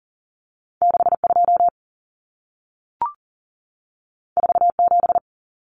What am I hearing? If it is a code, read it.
Morse code, heard as a keyed tone, about 40 wpm, 700 Hz. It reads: 62 47
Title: Numbers - 2-Digit 40wpm